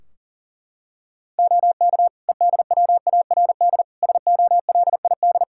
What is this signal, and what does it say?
Morse code, heard as a keyed tone, about 40 wpm, 700 Hz. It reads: OK EDWARD SOLID
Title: Practice QSO - Exchange 3 - More Info - 40wpm